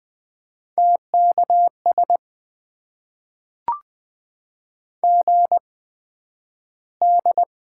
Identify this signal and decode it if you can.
Morse code, heard as a keyed tone, about 20 wpm, spaced as for 11 wpm, 700 Hz. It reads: TKS GD
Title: Top 90 QSO Elements - 8x Character Spacing - 20wpm